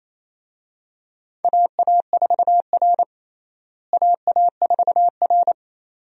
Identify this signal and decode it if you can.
Morse code, heard as a keyed tone, about 28 wpm, 700 Hz. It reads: AA4R AA4R
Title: USA Call Signs - 4 Characters - Repeated 2x - 28wpm